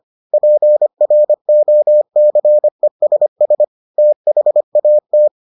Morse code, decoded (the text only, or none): PROCESS THAT